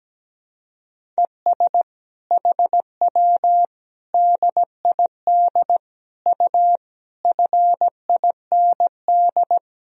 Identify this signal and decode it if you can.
Morse code, heard as a keyed tone, about 17 wpm, 700 Hz. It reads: ES HW DID U FIND
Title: Sentences from Top 100 Words and Abbreviations 17wpm